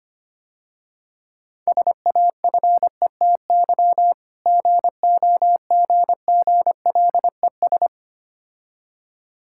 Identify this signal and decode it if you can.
Morse code, heard as a keyed tone, about 26 wpm, 700 Hz. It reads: SAFETY GOGGLES
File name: practice / 25wpm / Difficult Words - Speed-Racing with Hints - 25wpm